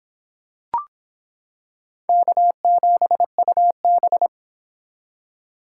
Morse code, heard as a keyed tone, about 26 wpm, 700 Hz. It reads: K7UB